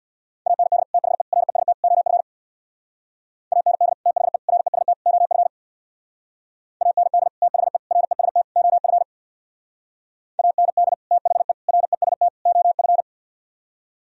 Morse code, heard as a keyed tone, about 62 wpm, 690 Hz. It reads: AND THE REST OF AND THE REST OF AND THE REST OF AND THE REST OF
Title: Speed-Racing - Sets of 4 Words 50wpm